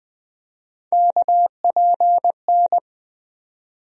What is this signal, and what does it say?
Morse code, heard as a keyed tone, about 20 wpm, 700 Hz. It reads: KPN
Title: USA Call Sign Postfixes - 3 Letters 20wpm